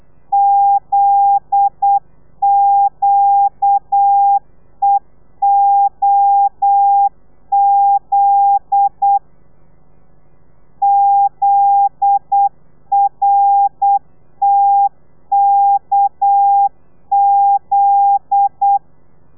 Morse code, heard as a keyed tone, about 8 wpm, 800 Hz. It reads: ZQEOZ ZRTKZ
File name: telegrafia / i2tfj / Lez9 35